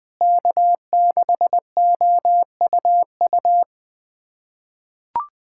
Morse code, heard as a keyed tone, about 20 wpm, 700 Hz. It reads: K6OUU
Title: USA Callsign DE Callsign - 5 Characters - 20wpm